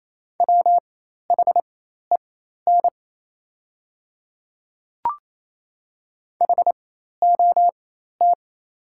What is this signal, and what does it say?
Morse code, heard as a keyed tone, about 28 wpm, spaced as for 13 wpm, 690 Hz. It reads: WHEN HOT
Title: 4x Character Spacing - Top 100 Words - 28wpm